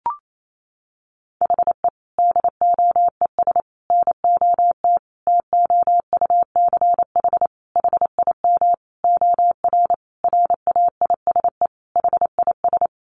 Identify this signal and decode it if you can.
Morse code, heard as a keyed tone, about 28 wpm, 700 Hz. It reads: HE DOES NOT TOUCH HIM OR RAISE HIS
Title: Sentences from Top 1400 Words 28wpm